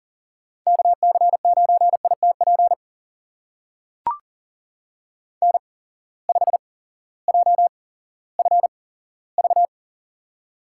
Morse code, heard as a keyed tone, about 42 wpm, spaced as for 18 wpm, 700 Hz. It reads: KC9ITP N5JFV
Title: USA Call Signs - 8x Character Spacing - 40wpm